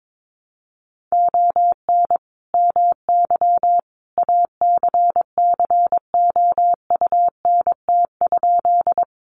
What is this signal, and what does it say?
Morse code, heard as a keyed tone, about 22 wpm, 700 Hz. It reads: ON MY ACCOUNT?
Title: Sentences from Top 900 Words 22wpm